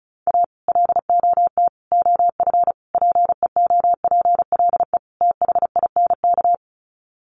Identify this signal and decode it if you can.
Morse code, heard as a keyed tone, about 34 wpm, 700 Hz. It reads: A LOT OF PEOPLE THINK
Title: Sets of 5 Words Encore 35wpm